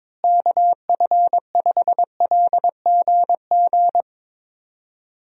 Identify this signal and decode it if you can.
Morse code, heard as a keyed tone, about 22 wpm, 700 Hz. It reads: KF5LGG
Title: USA Call Signs - 6 Characters - Repeated 2x - 22wpm